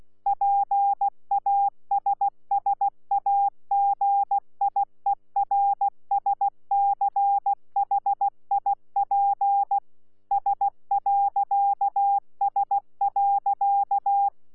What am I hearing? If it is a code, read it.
Morse code, heard as a keyed tone, about 16 wpm, 800 Hz. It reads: PASSAGIERSCHIP S.S.
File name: Z PI4VRZ / PI4VRZ 14-jun-2025